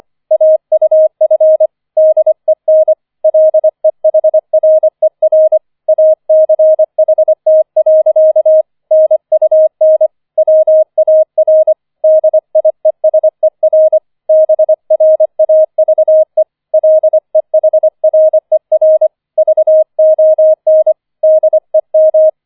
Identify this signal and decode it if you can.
Morse code, heard as a keyed tone, about 25 wpm, 630 Hz. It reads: AUF DEN LEHRER ACHT. NUN WAR DIESER BRAVE LEHRER VON DEM